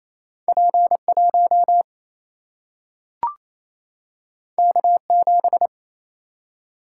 Morse code, heard as a keyed tone, about 28 wpm, 700 Hz. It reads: P1 K7
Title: Single-Letter Single-Number 28wpm